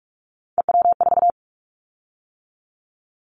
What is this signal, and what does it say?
Morse code, heard as a keyed tone, about 44 wpm, 700 Hz. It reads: EW4
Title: Callsign Prefixes for Top 50 DXCC - 45wpm